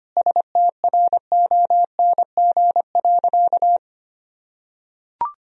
Morse code, heard as a keyed tone, about 25 wpm, 700 Hz. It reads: STRONG.
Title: Sentences from Top 1000 Words - Review Part 4 - 25wpm